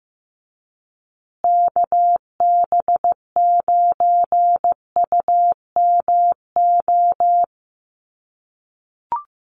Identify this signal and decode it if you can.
Morse code, heard as a keyed tone, about 15 wpm, 700 Hz. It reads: KB9UMO
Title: USA Call Signs Repeated 2x 15wpm